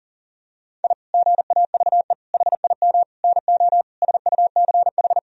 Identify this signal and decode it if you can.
Morse code, heard as a keyed tone, about 40 wpm, 700 Hz. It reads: I GAVE HIM NO SUCH
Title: Sentences from Top 1000 Words - Review Part 2 - 40wpm